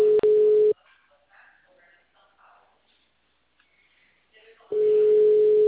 Ringback tone (Europe, 425 Hz).
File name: out- -2016- - - .43539